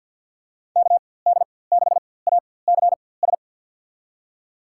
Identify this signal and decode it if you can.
Morse code, heard as a keyed tone, about 52 wpm, spaced as for 23 wpm, 700 Hz. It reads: KD6ACS
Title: USA Call Signs - 4x Character Spacing - 50wpm